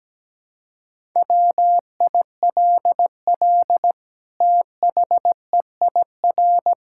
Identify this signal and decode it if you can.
Morse code, heard as a keyed tone, about 17 wpm, 700 Hz. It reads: WILL THEIR
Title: Sentences from Top 1000 Words - Review Part 6 - 17wpm